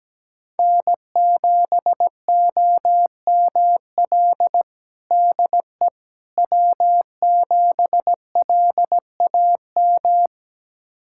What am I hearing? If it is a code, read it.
Morse code, heard as a keyed tone, about 17 wpm, 700 Hz. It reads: N7OML DE W7LAM
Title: USA Callsign DE Callsign - 5 Characters - 17wpm